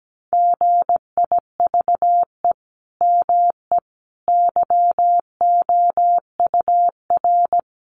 Morse code, heard as a keyed tone, about 17 wpm, 700 Hz. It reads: GIVE ME YOUR
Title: Sentences from Top 200 Words 17wpm